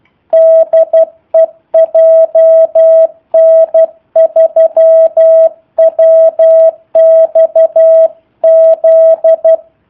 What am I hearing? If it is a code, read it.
Morse code, heard as a keyed tone, about 12 wpm, 650 Hz. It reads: DE JN3WXZ
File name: jh8cbh / 2019cw / sp03- w